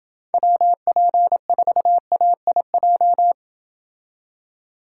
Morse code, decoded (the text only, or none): WP4AIJ